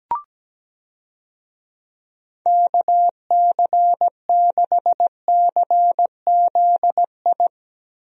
Morse code, heard as a keyed tone, about 17 wpm, 700 Hz. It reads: KC6CZI
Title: USA Call Signs - 6 Characters - Repeated 2x - 17wpm